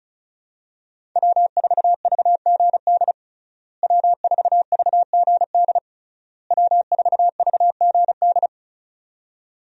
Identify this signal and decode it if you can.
Morse code, heard as a keyed tone, about 36 wpm, 700 Hz. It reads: W4VGD W4VGD W4VGD
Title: USA Call Signs - 5 Characters - Repeated 3x - 35wpm